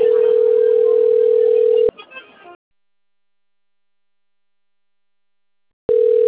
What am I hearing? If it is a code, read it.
Ringback tone (North America), 440 + 480 Hz.